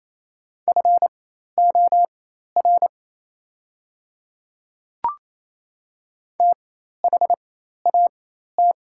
Morse code, heard as a keyed tone, about 28 wpm, spaced as for 13 wpm, 700 Hz. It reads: FOR THAT
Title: 4x Character Spacing - Top 100 Words - 28wpm